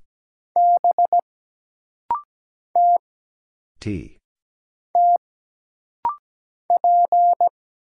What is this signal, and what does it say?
Morse code, heard as a keyed tone, about 17 wpm, 700 Hz. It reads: B T T P
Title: Single Letter-Number - Rapid-Fire 17wpm